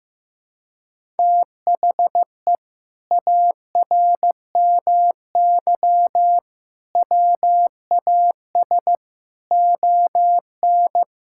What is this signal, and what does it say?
Morse code, heard as a keyed tone, about 15 wpm, 700 Hz. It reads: THE ARMY WAS ON
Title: Sentences from Top 1000 Words - Review Part 5 - 15wpm